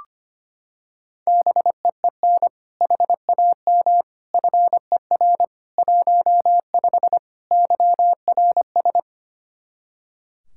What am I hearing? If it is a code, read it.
Morse code, heard as a keyed tone, about 25 wpm, 700 Hz. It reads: BEEN HAM FER 15 YRS
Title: K1USN SST QSO Practice 25wpm